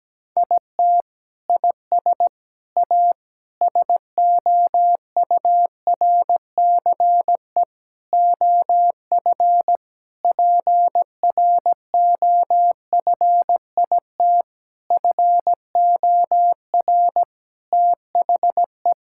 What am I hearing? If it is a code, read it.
Morse code, heard as a keyed tone, about 17 wpm, 700 Hz. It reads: IT IS A SOURCE OF PROFIT FOR THE
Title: Sentences from Top 1000 Words - Review Part 5 - 17wpm